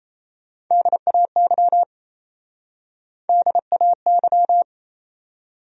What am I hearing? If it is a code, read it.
Morse code, heard as a keyed tone, about 34 wpm, 700 Hz. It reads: DAY DAY
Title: Speed-Racing Top 200 Words 28wpm